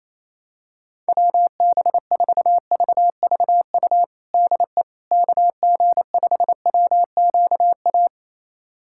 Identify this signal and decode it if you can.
Morse code, heard as a keyed tone, about 28 wpm, 700 Hz. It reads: WB4VVU DE KG5WQA
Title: USA Callsign DE Callsign - 6 Characters - 28wpm